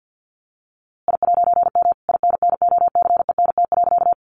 Morse code, heard as a keyed tone, about 50 wpm, 700 Hz. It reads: I'M INNOCENT.
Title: A Sound Of Thunder 50wpm